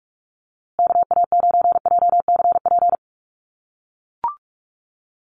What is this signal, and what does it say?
Morse code, heard as a keyed tone, about 44 wpm, 700 Hz. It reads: KA9JCP
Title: USA Call Signs Repeated 2x 45wpm